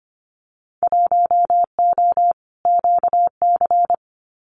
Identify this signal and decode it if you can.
Morse code, heard as a keyed tone, about 25 wpm, 700 Hz. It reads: 1O QC
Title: Winter Field Day Exchanges 25wpm